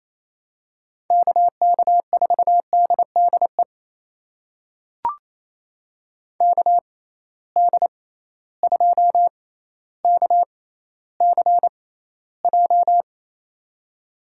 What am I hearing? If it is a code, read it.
Morse code, heard as a keyed tone, about 28 wpm, spaced as for 14 wpm, 700 Hz. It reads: KK4DDE KD2KCJ
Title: USA Call Signs - 6 Characters - 6x Character Spacing - 28wpm